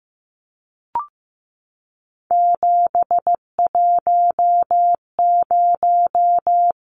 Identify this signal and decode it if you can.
Morse code, heard as a keyed tone, about 15 wpm, 700 Hz. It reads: 710
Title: Numbers - 3-Digit 15wpm